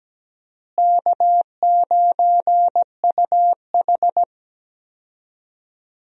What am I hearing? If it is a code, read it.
Morse code, heard as a keyed tone, about 17 wpm, 700 Hz. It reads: K9UH
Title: USA Call Signs - 4 Characters - Repeated 2x - 17wpm